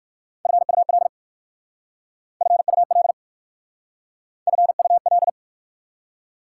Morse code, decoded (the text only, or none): FUL FUL FUL